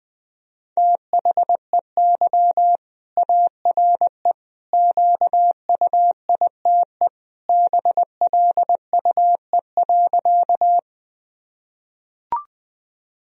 Morse code, heard as a keyed tone, about 20 wpm, 700 Hz. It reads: THEY ARE QUITE BLUE.
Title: Sentences from Top 1000 Words - Review Part 3 - 20wpm